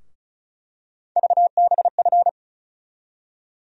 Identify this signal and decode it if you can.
Morse code, heard as a keyed tone, about 36 wpm, 700 Hz. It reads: VBF
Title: USA Call Sign Postfixes - 3 Letters 35wpm